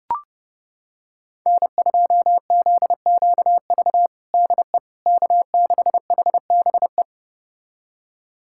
Morse code, heard as a keyed tone, about 30 wpm, 700 Hz. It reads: N2ZQV DE K6HBE
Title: USA Callsign DE Callsign 30wpm